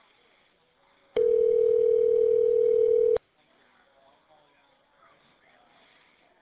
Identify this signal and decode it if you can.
Ringback tone (North America), 440 + 480 Hz.